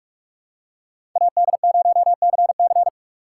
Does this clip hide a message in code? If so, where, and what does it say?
Yes: Morse code from 1.2 to 2.9 s, heard as a keyed tone, about 44 wpm, 700 Hz. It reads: AD0CC